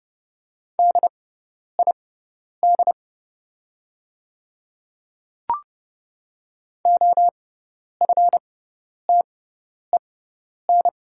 Morse code, heard as a keyed tone, about 31 wpm, spaced as for 10 wpm, 700 Hz. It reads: DID OFTEN
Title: 6x Character Spacing - Top 300 Words - 30wpm